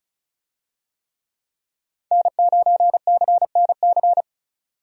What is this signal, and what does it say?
Morse code, heard as a keyed tone, about 34 wpm, 700 Hz. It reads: N9CNC
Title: USA Call Signs - 5 Characters - 35wpm